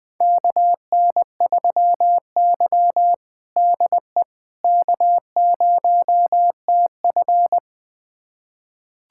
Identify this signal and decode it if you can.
Morse code, heard as a keyed tone, about 20 wpm, 700 Hz. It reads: KN3Y DE K0TF
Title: USA Callsign DE Callsign - 4 Characters - 20wpm